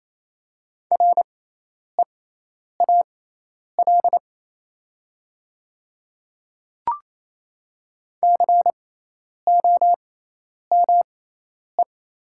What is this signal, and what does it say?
Morse code, heard as a keyed tone, about 28 wpm, spaced as for 10 wpm, 700 Hz. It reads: REAL COME
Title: 6x Character Spacing - Top 300 Words - 28wpm